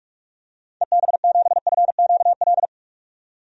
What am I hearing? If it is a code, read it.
Morse code, heard as a keyed tone, about 46 wpm, 700 Hz. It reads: EB7FQL
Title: DXCC-100 Call Signs - 6 Characters - 45wpm